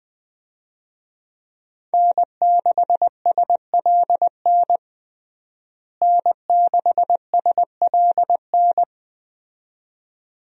Morse code, heard as a keyed tone, about 21 wpm, 700 Hz. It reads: N6SLN N6SLN
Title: USA Call Signs - 5 Characters - Repeated 2x - 20wpm